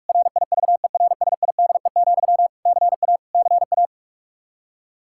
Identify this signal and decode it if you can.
Morse code, heard as a keyed tone, about 46 wpm, 700 Hz. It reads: RIVERSIDE, CA CA
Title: Practice QSO - Exchange 1 - Beginning - 45wpm